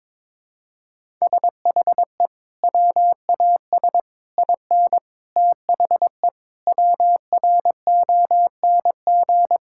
Morse code, heard as a keyed tone, about 22 wpm, 700 Hz. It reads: SHE WAS IN THE WRONG
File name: Sentences with 9-Word Phrases - 22wpm